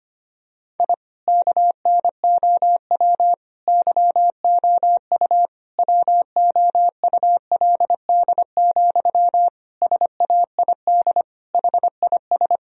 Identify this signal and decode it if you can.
Morse code, heard as a keyed tone, about 25 wpm, 700 Hz. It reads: I KNOW YOU WOULD, SAID HIS
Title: Sentences from Top 1000 Words - Review Part 1 - 25wpm